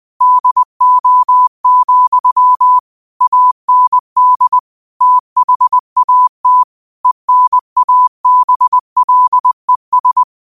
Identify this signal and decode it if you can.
Morse code, heard as a keyed tone, about 20 wpm, 1000 Hz. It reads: DO, AND THAT ENABLES
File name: qotdfiles / 20wpm